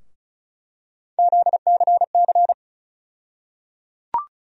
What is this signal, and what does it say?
Morse code, heard as a keyed tone, about 36 wpm, 700 Hz. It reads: ZCC
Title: USA Call Sign Postfixes - 3 Letters 35wpm